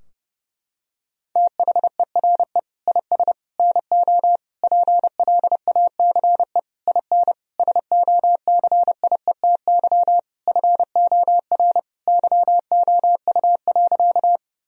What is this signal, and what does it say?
Morse code, heard as a keyed tone, about 30 wpm, 700 Hz. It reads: THERE IS NO PLACE IN SOCIETY FOR YOU.
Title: Sentences from Top 1000 Words - Review Part 4 - 30wpm